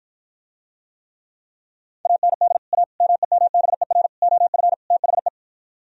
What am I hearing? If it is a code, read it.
Morse code, heard as a keyed tone, about 52 wpm, 700 Hz. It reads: AND A MEMBER OF THE